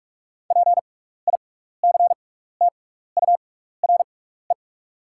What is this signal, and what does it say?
Morse code, heard as a keyed tone, about 44 wpm, spaced as for 15 wpm, 700 Hz. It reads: PICTURE
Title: 6x Character Spacing - Top 200 Words - 45wpm